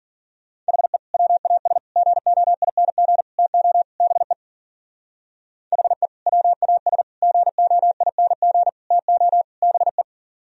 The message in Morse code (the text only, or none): HE WAS GOING TO BE HE WAS GOING TO BE